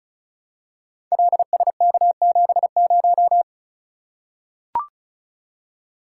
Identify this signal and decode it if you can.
Morse code, heard as a keyed tone, about 36 wpm, 700 Hz. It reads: LSK70
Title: Code Groups - 5 Letter:Numbers - 35wpm